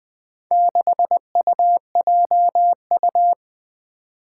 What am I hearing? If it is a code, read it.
Morse code, heard as a keyed tone, about 20 wpm, 700 Hz. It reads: 6UJU
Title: Four Challenging Letter-Number Combinations 20wpm